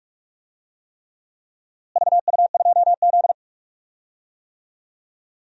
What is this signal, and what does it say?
Morse code, heard as a keyed tone, about 46 wpm, 700 Hz. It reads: VU2Z
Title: DXCC-100 Call Signs - 4 Characters - Repeated 3x - 45wpm